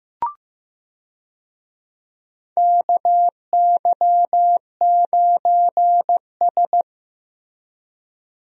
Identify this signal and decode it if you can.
Morse code, heard as a keyed tone, about 15 wpm, 700 Hz. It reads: KY9S
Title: USA Call Signs - 4 Characters - Repeated 2x - 15wpm